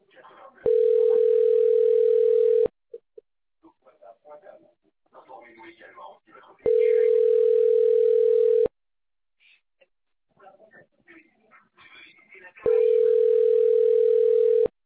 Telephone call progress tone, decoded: ringback tone